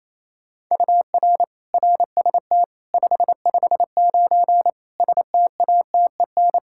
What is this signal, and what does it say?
Morse code, heard as a keyed tone, about 28 wpm, 700 Hz. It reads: UR RST 559 STATEN
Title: Practice QSO - Exchange 1 - Beginning - 28wpm